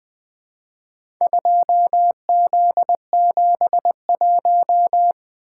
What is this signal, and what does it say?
Morse code, heard as a keyed tone, about 21 wpm, 700 Hz. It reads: 2Z71
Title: Four Letter-Number Combinations - 20wpm